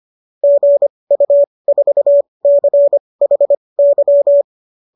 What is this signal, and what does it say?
Morse code, heard as a keyed tone, about 25 wpm, spaced as for 20 wpm, 570 Hz. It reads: GU4CHY